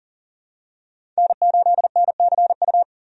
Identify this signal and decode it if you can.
Morse code, heard as a keyed tone, about 40 wpm, 700 Hz. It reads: N8NCU
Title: USA Call Signs - 5 Characters - 40wpm